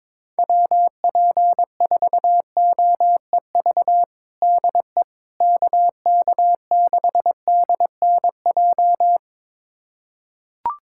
Morse code, heard as a keyed tone, about 22 wpm, 700 Hz. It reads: WP4OEV DE KK6DNJ
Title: USA Callsign DE Callsign - 6 Characters - 22wpm